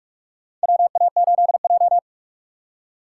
Morse code, heard as a keyed tone, about 45 wpm, 700 Hz. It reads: WA8J
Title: USA Call Signs - 4 Characters - 4x Character Spacing - 45wpm